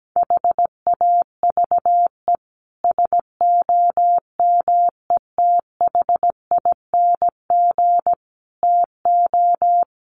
Morse code, heard as a keyed tone, about 17 wpm, 690 Hz. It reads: HAVE SOMETHING TO
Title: Sets of 5 Words Encore 17wpm